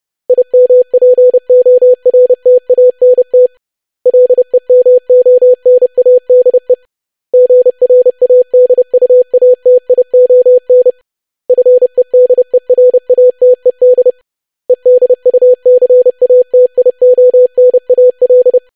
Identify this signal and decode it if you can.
Morse code, heard as a keyed tone, about 30 wpm, 500 Hz. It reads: IMPORTANT LEMONADE GRADUATION FEDERATED EDUCATIONAL
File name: lesson / web / longwords-25wpm